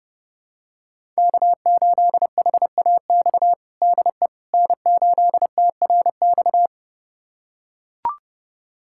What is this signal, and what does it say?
Morse code, heard as a keyed tone, about 31 wpm, 700 Hz. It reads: K8HAX DE N8TRX
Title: USA Callsign DE Callsign - 5 Characters - 30wpm